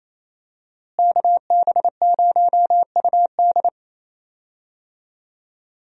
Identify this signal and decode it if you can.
Morse code, heard as a keyed tone, about 28 wpm, 700 Hz. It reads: KB0UD
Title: USA Call Signs - 5 Characters - Repeated 3x - 28wpm